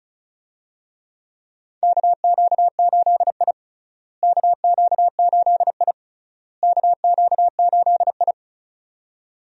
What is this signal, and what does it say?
Morse code, heard as a keyed tone, about 36 wpm, 700 Hz. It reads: KQ8I KQ8I KQ8I
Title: USA Call Signs - 4 Characters - Repeated 3x - 35wpm